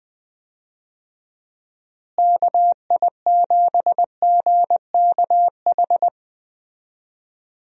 Morse code, heard as a keyed tone, about 20 wpm, 700 Hz. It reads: KI7GKH
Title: USA Call Signs - 6 Characters - Repeated 2x - 20wpm